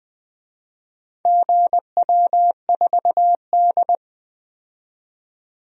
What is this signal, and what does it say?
Morse code, heard as a keyed tone, about 20 wpm, 700 Hz. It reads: GW4D
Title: DXCC-100 Call Signs - 4 Characters - 20wpm